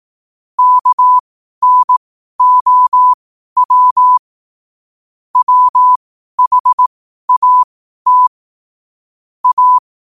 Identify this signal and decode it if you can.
Morse code, heard as a keyed tone, about 18 wpm, spaced as for 12 wpm, 1000 Hz. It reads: KNOW WHAT A
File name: qotdfiles / 12wpm